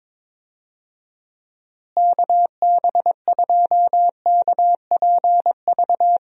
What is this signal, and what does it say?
Morse code, heard as a keyed tone, about 22 wpm, 700 Hz. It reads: KB2KPV
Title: USA Call Signs - 6 Characters - Repeated 3x - 22wpm